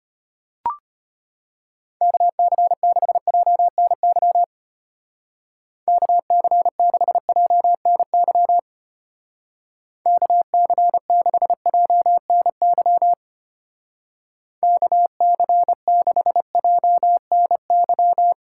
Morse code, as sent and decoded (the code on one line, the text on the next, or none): -.- -.-. -.... .--- -. -.-- / -.- -.-. -.... .--- -. -.-- / -.- -.-. -.... .--- -. -.-- / -.- -.-. -.... .--- -. -.--
KC6JNY KC6JNY KC6JNY KC6JNY